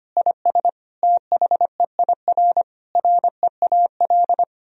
Morse code, heard as a keyed tone, about 25 wpm, 700 Hz. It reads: IS THEIR REAL
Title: Sentences from Top 1000 Words - Review Part 1 - 25wpm